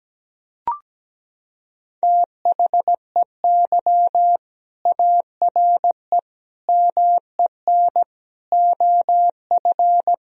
Morse code, heard as a keyed tone, about 17 wpm, 700 Hz. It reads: THEY ARE MEN OF
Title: Sentences from Top 1000 Words - Review Part 2 - 17wpm